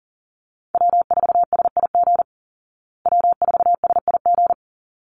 Morse code, heard as a keyed tone, about 37 wpm, 700 Hz. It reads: W4SIG W4SIG
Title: Call Signs Repeated 3x - 40wpm